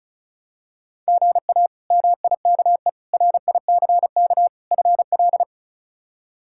Morse code, heard as a keyed tone, about 34 wpm, 700 Hz. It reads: GA MIKE RICK FL
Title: K1USN SST QSO Practice - 35wpm